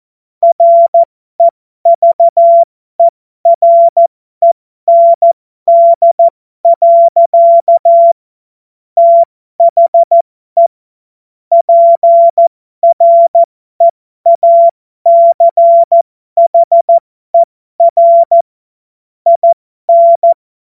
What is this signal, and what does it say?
Morse code, heard as a keyed tone, about 14 wpm, spaced as for 12 wpm, 680 Hz. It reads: REVEREND. THE PREACHER IN